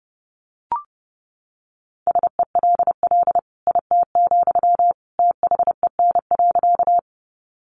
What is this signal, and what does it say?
Morse code, heard as a keyed tone, about 30 wpm, 700 Hz. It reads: SELL IT, THEN.
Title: Sentences from Top 1100 Words 30wpm